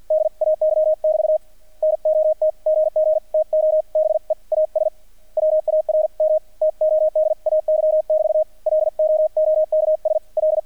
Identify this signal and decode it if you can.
Morse code, heard as a keyed tone, about 48 wpm, 620 Hz, with radio noise. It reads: DAY<BT> NOT GG TO BE AS WARM TODAY<BT> LOOKS L
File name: archived / CRSnet